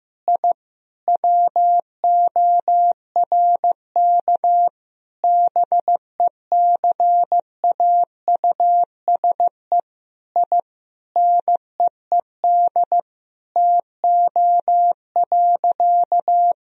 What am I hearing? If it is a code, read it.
Morse code, heard as a keyed tone, about 15 wpm, 700 Hz. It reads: I WORK BECAUSE I NEED TO.
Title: Sentences from Top 1000 Words - Review Part 3 - 15wpm